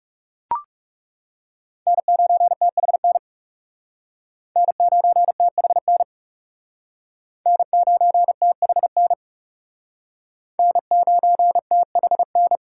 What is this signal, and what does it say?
Morse code, heard as a keyed tone, about 37 wpm, 700 Hz. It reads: N9THN N9THN N9THN N9THN
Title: Speed Racing - USA 1x3 Call Signs 30wpm